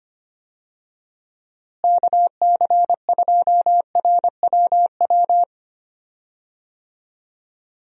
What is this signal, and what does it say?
Morse code, heard as a keyed tone, about 25 wpm, 700 Hz. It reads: KC2RWW
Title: USA Call Signs - 6 Characters - Repeated 2x - 25wpm